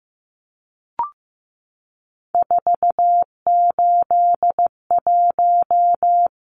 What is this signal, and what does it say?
Morse code, heard as a keyed tone, about 15 wpm, 700 Hz. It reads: 481
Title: Numbers - 3-Digit 15wpm